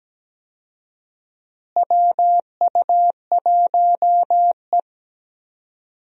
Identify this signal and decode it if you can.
Morse code, heard as a keyed tone, about 17 wpm, 700 Hz. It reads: WU1E
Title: USA Call Signs - 4 Characters - Repeated 3x - 17wpm